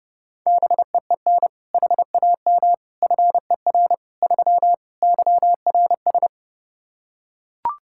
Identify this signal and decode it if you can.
Morse code, heard as a keyed tone, about 30 wpm, 700 Hz. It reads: BEEN HAM FER 3 YRS
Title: K1USN SST QSO Practice 30wpm